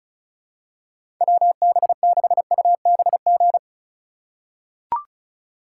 Morse code, heard as a keyed tone, about 36 wpm, 700 Hz. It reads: WB6UBG
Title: USA Call Signs - 6x Character Spacing - 35wpm